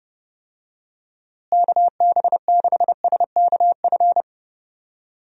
Morse code, heard as a keyed tone, about 31 wpm, 700 Hz. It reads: KB6SKF
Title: USA Call Signs - 6 Characters - Repeated 2x - 30wpm